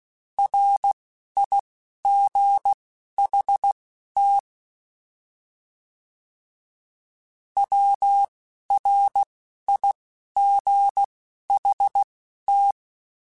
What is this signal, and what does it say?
Morse code, heard as a keyed tone, about 16 wpm, spaced as for 11 wpm, 800 Hz. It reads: RIGHT WRIGHT
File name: Short Runs / Short Runs-mp3s / Words @10